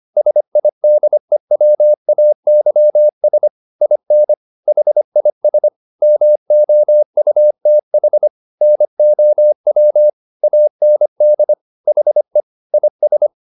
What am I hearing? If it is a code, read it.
Morse code, heard as a keyed tone, about 25 wpm, 600 Hz. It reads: SIDEWAYS IN HIS MOUTH NOW AND HE IS
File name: the old man and the sea part 1 25wpm